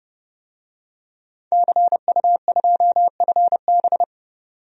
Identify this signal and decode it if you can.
Morse code, heard as a keyed tone, about 30 wpm, 700 Hz. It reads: CU2FB
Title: DXCC-100 Call Signs - 5 Characters - 30wpm